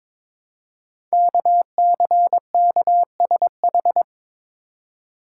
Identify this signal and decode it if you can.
Morse code, heard as a keyed tone, about 22 wpm, 700 Hz. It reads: KCKSH
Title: Code Groups - 5 Letters - 22wpm